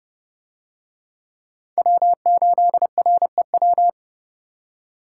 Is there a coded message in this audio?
Yes: Morse code from 1.8 to 3.9 s, heard as a keyed tone, about 29 wpm, 700 Hz. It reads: W8REW